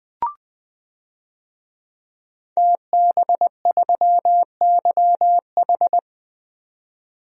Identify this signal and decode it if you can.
Morse code, heard as a keyed tone, about 20 wpm, 700 Hz. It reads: TB3YH
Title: DXCC-100 Call Signs - 5 Characters - 20wpm